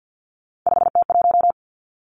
Morse code, heard as a keyed tone, about 50 wpm, 700 Hz. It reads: 5T1
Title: Three Challenging Letter-Number Combinations 50wpm